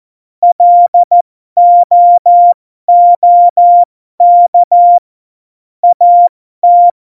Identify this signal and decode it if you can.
Morse code, heard as a keyed tone, about 14 wpm, spaced as for 12 wpm, 700 Hz. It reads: LOOK AT
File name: Morsekurs / Witze